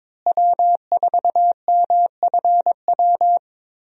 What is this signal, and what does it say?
Morse code, heard as a keyed tone, about 22 wpm, 700 Hz. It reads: W4MFW
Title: USA Call Signs - 5 Characters - Repeated 2x - 22wpm